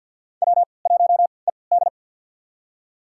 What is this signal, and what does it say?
Morse code, heard as a keyed tone, about 52 wpm, spaced as for 29 wpm, 700 Hz. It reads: W1ED
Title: USA Call Signs - 4 Characters - 3x Character Spacing - 50wpm